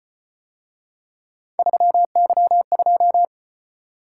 Morse code, heard as a keyed tone, about 33 wpm, 700 Hz. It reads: 3Y2